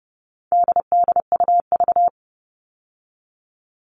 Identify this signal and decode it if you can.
Morse code, heard as a keyed tone, about 30 wpm, 700 Hz. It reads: DDUV
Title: All Cut Numbers - 4-Digit 30wpm